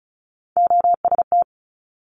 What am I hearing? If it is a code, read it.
Morse code, heard as a keyed tone, about 36 wpm, 700 Hz. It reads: OST
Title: Multi-letter Phonemes 35wpm